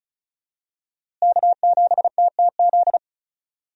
Morse code, heard as a keyed tone, about 36 wpm, 700 Hz. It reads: K7TTZ
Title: USA Call Signs - 5 Characters - Repeated 3x - 35wpm